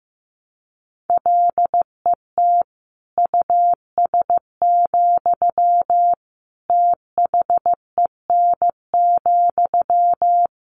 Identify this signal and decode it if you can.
Morse code, heard as a keyed tone, about 14 wpm, 700 Hz. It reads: LET US, THEN,
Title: Sentences from Top 1000 Words 15wpm